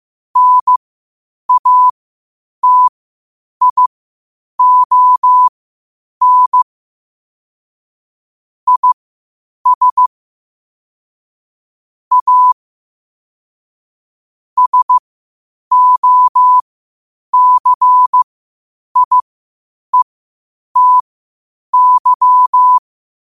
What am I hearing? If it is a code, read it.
Morse code, heard as a keyed tone, about 15 wpm, spaced as for 8 wpm, 1000 Hz. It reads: NATION IS A SOCIETY
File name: qotdfiles / 8wpm